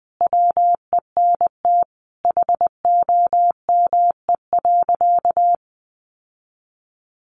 Morse code, heard as a keyed tone, about 20 wpm, 700 Hz. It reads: WENT HOME.
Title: Sentences from Top 1200 Words 20wpm